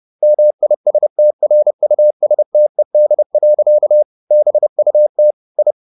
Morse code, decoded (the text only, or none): MISTRUSTED. BUT I